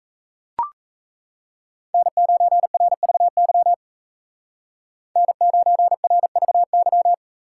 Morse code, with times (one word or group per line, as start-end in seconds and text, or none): 1.9-3.7 N9RVY
5.2-7.1 N9RVY